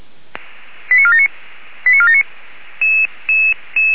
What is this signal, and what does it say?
Five-tone selective call (ZVEI), heard as quick runs of five tones, tones of about 70 ms. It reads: 89479 89479